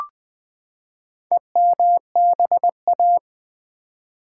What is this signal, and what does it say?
Morse code, heard as a keyed tone, about 20 wpm, 700 Hz. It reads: EMBA
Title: Build A Word - 4 Letters At A Time - 12-Letter Words - 20wpm